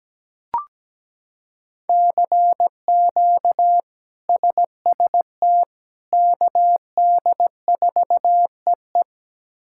Morse code, heard as a keyed tone, about 17 wpm, 700 Hz. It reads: CQ SST KD4EE
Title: K1USN SST QSO Practice - 17wpm